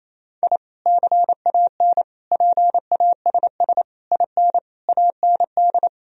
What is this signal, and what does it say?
Morse code, heard as a keyed tone, about 28 wpm, 700 Hz. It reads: I CAN PASS IN AND
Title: Sentences from Top 1000 Words - Review Part 4 - 28wpm